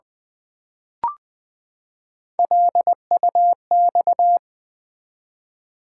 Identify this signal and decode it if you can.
Morse code, heard as a keyed tone, about 20 wpm, 700 Hz. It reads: LUX
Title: Build A Word - 3 Letters At A Time - 6-Letter Words - 20wpm